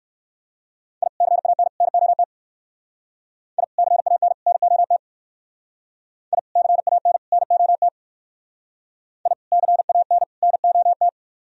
Morse code, heard as a keyed tone, about 57 wpm, 700 Hz. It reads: I CAN NOT I CAN NOT I CAN NOT I CAN NOT